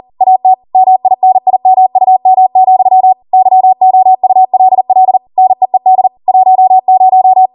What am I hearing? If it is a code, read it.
Morse code, heard as a keyed tone, about 40 wpm, 750 Hz. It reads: AT MINIMUM, YOULL NEED 10